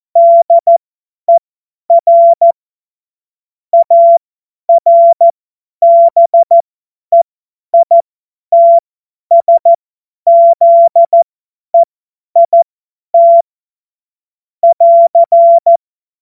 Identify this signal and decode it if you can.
Morse code, heard as a keyed tone, about 14 wpm, spaced as for 10 wpm, 680 Hz. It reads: DER ARBEITSZEIT <AR>
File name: Morsekurs / Witze / Witz 23. Dezember 2025 50 F70